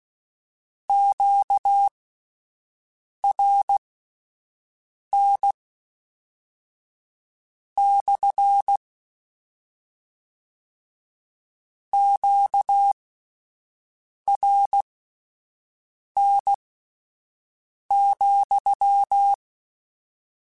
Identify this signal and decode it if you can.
Morse code, heard as a keyed tone, about 16 wpm, spaced as for 5 wpm, 800 Hz. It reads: QRN / QRN,
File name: Q Calls / Q Calls-mp3s / Q CALLS SPSL @10